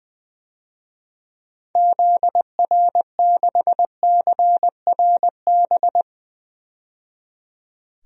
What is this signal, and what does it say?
Morse code, heard as a keyed tone, about 20 wpm, 700 Hz. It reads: ZR6CRB
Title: DXCC-100 Call Signs - 6 Characters - 20wpm